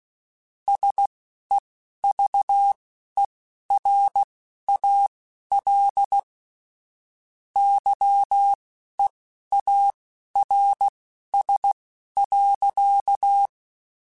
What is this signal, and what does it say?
Morse code, heard as a keyed tone, about 16 wpm, spaced as for 11 wpm, 800 Hz. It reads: SEVERAL YEARS.
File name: QSOs / QSOs-mp3s / kc7haa@10